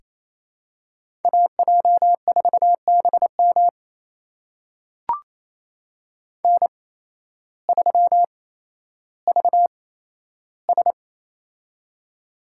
Morse code, heard as a keyed tone, about 28 wpm, spaced as for 13 wpm, 700 Hz. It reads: AJ4BM N3VS
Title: USA Call Signs - 8x Character Spacing - 28wpm